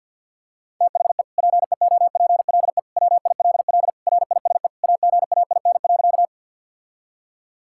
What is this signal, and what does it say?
Morse code, heard as a keyed tone, about 52 wpm, 700 Hz. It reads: THE PEOPLE WILL RISE AGAIN.
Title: Sentences from Top 1000 Words - Review Part 6 - 50wpm